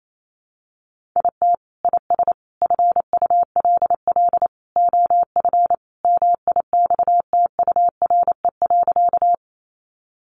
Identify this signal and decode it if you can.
Morse code, heard as a keyed tone, about 28 wpm, 700 Hz. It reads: IT IS FULL OF MIXTURE.
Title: Sentences from Top 1500 Words 28wpm